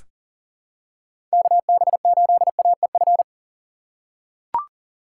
Morse code, heard as a keyed tone, about 40 wpm, 700 Hz. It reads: KB8AEF
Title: USA Call Signs - 6x Character Spacing - 40wpm